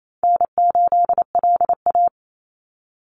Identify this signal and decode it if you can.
Morse code, heard as a keyed tone, about 28 wpm, 700 Hz. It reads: N8LA
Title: Easy Call Signs - 28wpm